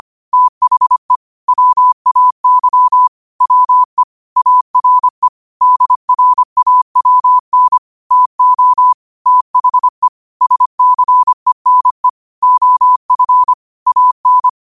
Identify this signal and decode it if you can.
Morse code, heard as a keyed tone, about 25 wpm, 1000 Hz. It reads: THE WAY WE ARE DRAWN TO THE SCENE OF AN
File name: qotdfiles / 25wpm